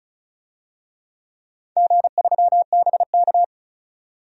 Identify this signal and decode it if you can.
Morse code, heard as a keyed tone, about 36 wpm, 700 Hz. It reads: G3BK
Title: DXCC-100 Call Signs - 4 Characters - 35wpm